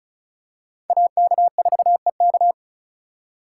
Morse code, heard as a keyed tone, about 34 wpm, 700 Hz. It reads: AK4EK
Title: USA Call Signs - 5 Characters - 2x Character Spacing - 35wpm